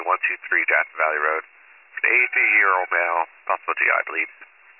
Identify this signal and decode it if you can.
Emergency Alert System (SAME) tones, heard as harsh buzzing bursts, with speech. It is heard from 0.4 to 4.5 s.